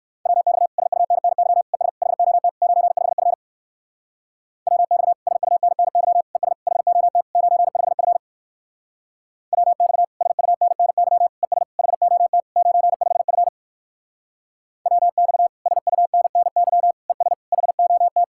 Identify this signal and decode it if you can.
Morse code, heard as a keyed tone, about 62 wpm, 700 Hz. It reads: WX SUNNY ES HOT 95F WX SUNNY ES HOT 95F WX SUNNY ES HOT 95F WX SUNNY ES HOT